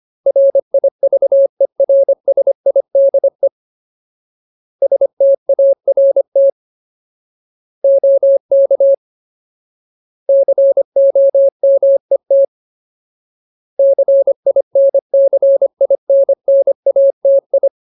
Morse code, heard as a keyed tone, about 25 wpm, 550 Hz. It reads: RIVERSIDE START OK COMET CINCINNATI
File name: Ultimate Word List / 25WPM Ultimate